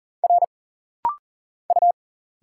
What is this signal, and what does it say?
Morse code, heard as a keyed tone, about 40 wpm, 700 Hz. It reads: R U
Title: Single Letter-Number Alt - Rapid-Fire 40wpm